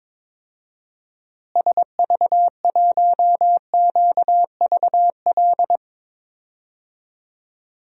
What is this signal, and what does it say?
Morse code, heard as a keyed tone, about 22 wpm, 700 Hz. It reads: SV1QVL
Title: DXCC-100 Call Signs - 6 Characters - Repeated 2x - 22wpm